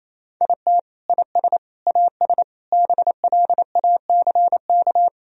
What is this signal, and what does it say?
Morse code, heard as a keyed tone, about 27 wpm, 700 Hz. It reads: IT IS AS BLACK
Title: Sentences from Top 1000 Words - Review Part 2 - 28wpm